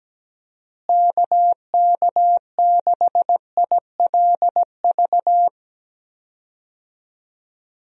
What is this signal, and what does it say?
Morse code, heard as a keyed tone, about 17 wpm, 700 Hz. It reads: KK6ILV
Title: USA Call Signs - 6 Characters - Repeated 2x - 17wpm